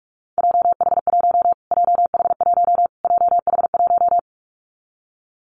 Morse code, heard as a keyed tone, about 44 wpm, 700 Hz. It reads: JH1 JH1 JH1
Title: Callsign Prefixes for Top 25 DXCC - Repeated 3x - 45wpm